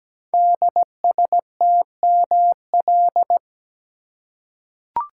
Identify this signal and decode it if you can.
Morse code, heard as a keyed tone, about 17 wpm, 700 Hz. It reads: DSTML
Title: Code Groups - 5 Letters - 17wpm